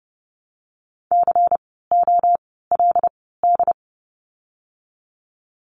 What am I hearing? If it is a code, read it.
Morse code, heard as a keyed tone, about 30 wpm, spaced as for 17 wpm, 700 Hz. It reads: COLD
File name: Lesson 16f - Letter c - Words 30wpm